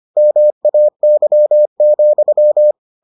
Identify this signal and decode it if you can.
Morse code, heard as a keyed tone, about 25 wpm, 600 Hz. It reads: MAY,